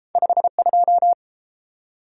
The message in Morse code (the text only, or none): 52